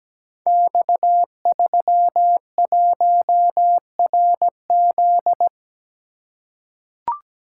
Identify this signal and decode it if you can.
Morse code, heard as a keyed tone, about 17 wpm, 700 Hz. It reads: X31RZ
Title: Code Groups - 5 Letter:Numbers - 17wpm